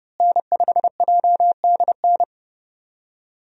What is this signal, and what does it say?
Morse code, heard as a keyed tone, about 30 wpm, 700 Hz. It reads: N5JDN
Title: USA Call Signs - 5 Characters - Repeated 3x - 30wpm